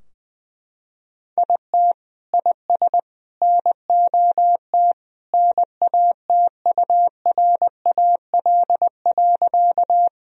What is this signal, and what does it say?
Morse code, heard as a keyed tone, about 20 wpm, 700 Hz. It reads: IT IS NOT NATURAL.
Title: Sentences from Top 1000 Words - Review Part 4 - 20wpm